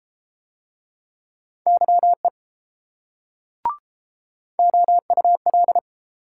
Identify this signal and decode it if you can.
Morse code, heard as a keyed tone, about 30 wpm, 700 Hz. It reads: YE OUL